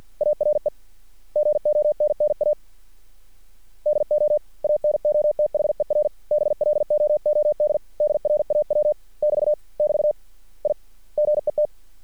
Morse code, heard as a keyed tone, about 50 wpm, 600 Hz. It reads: ARE GONNA DO ANOTHER BLOOD DRAW <BT> <BT> I GET
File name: archived / CRSnet